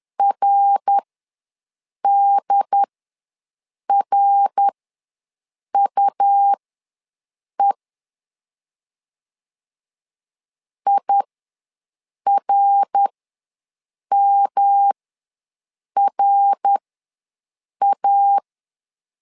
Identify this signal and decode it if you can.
Morse code, heard as a keyed tone, about 11 wpm, spaced as for 6 wpm, 800 Hz. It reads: RDRUE IRMRA
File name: LEZIONI / 3 MWRO / 3 MWR ws 10wpm 2151